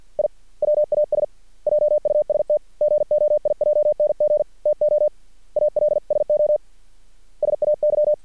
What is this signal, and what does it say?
Morse code, heard as a keyed tone, about 50 wpm, 600 Hz. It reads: I WAS JUST GOIJNG TO ALSO SAY
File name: archived / CRSnet